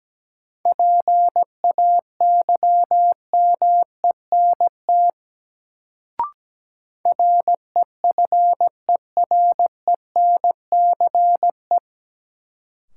Morse code, heard as a keyed tone, about 17 wpm, 700 Hz. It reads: PAYMENT REFERENCE
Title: Sentences from Top 1200 Words - New Word Drill Alt - 17wpm